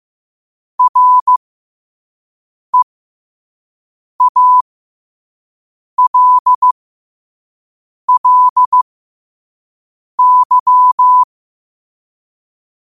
Morse code, heard as a keyed tone, about 15 wpm, spaced as for 5 wpm, 1000 Hz. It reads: REALLY
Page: Quotes for Fri, 15 Aug 2025 in Morse Code at 5 words per minute.